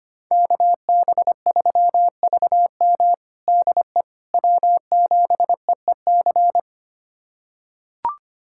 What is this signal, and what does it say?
Morse code, heard as a keyed tone, about 25 wpm, 700 Hz. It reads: KB3VM DE W7EEC
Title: USA Callsign DE Callsign - 5 Characters - 25wpm